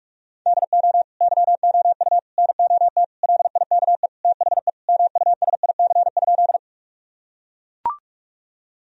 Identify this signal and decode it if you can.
Morse code, heard as a keyed tone, about 42 wpm, 700 Hz. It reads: DO YOU NOT LIKE THE MUSIC?
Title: Sentences from Top 1000 Words - Review Part 1 - 45wpm